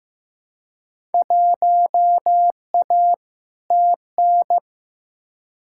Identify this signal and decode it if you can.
Morse code, heard as a keyed tone, about 15 wpm, 700 Hz. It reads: 1A TN
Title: Field Day Exchanges - 15wpm